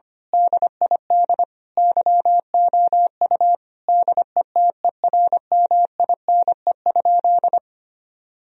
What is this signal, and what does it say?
Morse code, heard as a keyed tone, about 25 wpm, 700 Hz. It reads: DID YOU DETERMINE?
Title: Sentences from Top 1000 Words - Review Part 4 - 25wpm